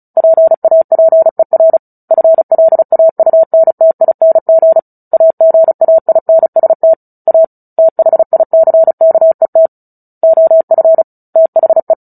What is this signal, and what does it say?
Morse code, heard as a keyed tone, about 34 wpm, 650 Hz. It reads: PAPER FLAUNTING AGAINST A THICKET OF THE
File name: Book Two / 09-Chapter 9 - 35 WPM